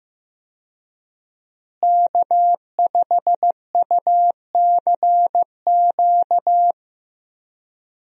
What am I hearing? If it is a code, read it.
Morse code, heard as a keyed tone, about 15 wpm, 700 Hz. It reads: K5UCQ
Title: USA Call Signs - 5 Characters - Repeated 3x - 15wpm